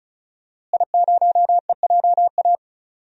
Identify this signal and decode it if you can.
Morse code, heard as a keyed tone, about 36 wpm, 700 Hz. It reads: I0EJA
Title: DXCC-100 Call Signs - 5 Characters - Repeated 3x - 35wpm